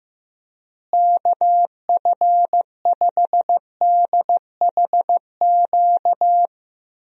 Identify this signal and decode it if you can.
Morse code, heard as a keyed tone, about 16 wpm, 700 Hz. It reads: KF5DHQ
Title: USA Call Signs - 6 Characters - Repeated 3x - 15wpm